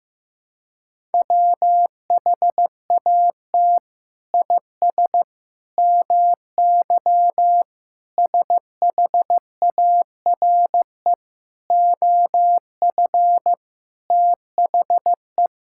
Morse code, heard as a keyed tone, about 16 wpm, 700 Hz. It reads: WHAT IS MY SHARE OF THE
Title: Sentences from Top 1000 Words - Review Part 6 - 15wpm